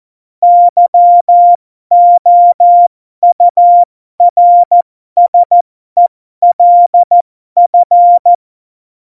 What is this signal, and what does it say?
Morse code, heard as a keyed tone, about 14 wpm, spaced as for 12 wpm, 700 Hz. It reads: YOURSELF
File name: Morsekurs / Witze